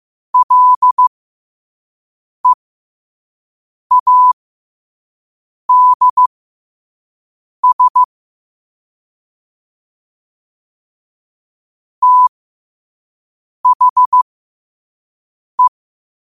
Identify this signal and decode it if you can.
Morse code, heard as a keyed tone, about 15 wpm, spaced as for 5 wpm, 1000 Hz. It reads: LEADS THE
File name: qotdfiles / 5wpm